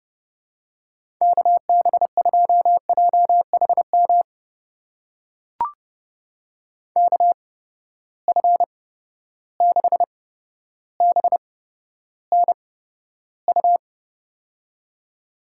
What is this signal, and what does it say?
Morse code, heard as a keyed tone, about 30 wpm, spaced as for 13 wpm, 700 Hz. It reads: KB2JHM KF6BNU
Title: USA Call Signs - 6 Characters - 8x Character Spacing - 30wpm